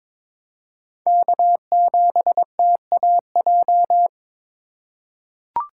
Morse code, heard as a keyed tone, about 22 wpm, 700 Hz. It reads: K7TAJ
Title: USA Call Signs - 4x Character Spacing - 22wpm